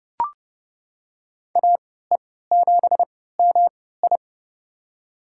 Morse code, heard as a keyed tone, about 30 wpm, spaced as for 17 wpm, 700 Hz. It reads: AE7MI
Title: USA Call Signs - 5 Characters - 3x Character Spacing - 30wpm